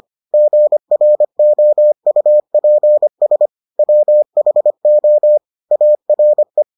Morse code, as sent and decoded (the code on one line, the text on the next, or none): --. .-. --- ..- .--. ... / .-- .... --- / .- .-. .
GROUPS WHO ARE